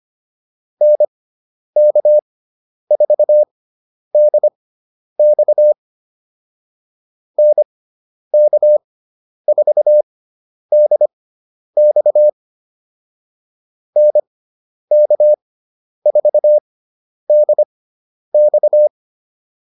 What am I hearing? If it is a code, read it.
Morse code, heard as a keyed tone, about 25 wpm, spaced as for 10 wpm, 600 Hz. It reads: NK4DX NK4DX NK4DX